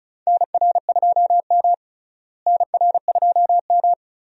Morse code, heard as a keyed tone, about 36 wpm, 700 Hz. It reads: NR2M NR2M
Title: USA Call Signs - 4 Characters - Repeated 3x - 35wpm